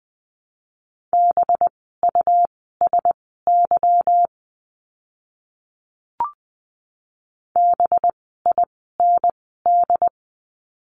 Morse code, heard as a keyed tone, about 20 wpm, spaced as for 14 wpm, 700 Hz. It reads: BUSY BIND
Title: 185 Lesson 30f - Letter b - Words 20wpm